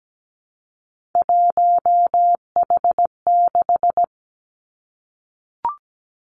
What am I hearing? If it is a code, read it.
Morse code, heard as a keyed tone, about 17 wpm, 700 Hz. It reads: 1H6
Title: Three Challenging Letter-Number Combinations 17wpm